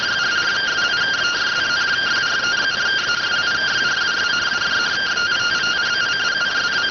Two-tone teleprinter signal, about 170 Hz shift.